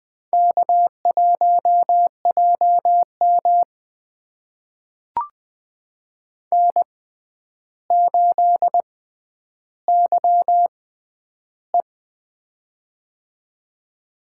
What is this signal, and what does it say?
Morse code, heard as a keyed tone, about 20 wpm, spaced as for 10 wpm, 700 Hz. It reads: K1JM N8YE
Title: USA Call Signs - 4 Characters - 6x Character Spacing - 20wpm